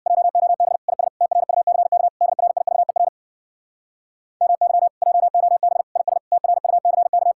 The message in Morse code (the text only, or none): JOB IS TRUCK DRIVER MY JOB IS TRUCK